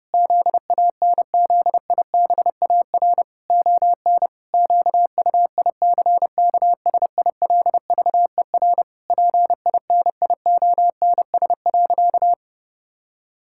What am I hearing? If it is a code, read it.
Morse code, heard as a keyed tone, about 30 wpm, 700 Hz. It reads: ZANZIBAR ON QUICKSILVER PINIONS.
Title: Pangram Challenge v2 - 30wpm